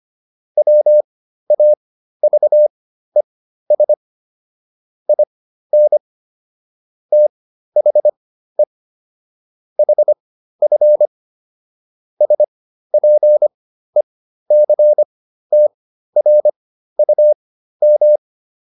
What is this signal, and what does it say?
Morse code, heard as a keyed tone, about 25 wpm, spaced as for 13 wpm, 600 Hz. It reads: WAVES IN THE HF SPECTRUM
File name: SS211 13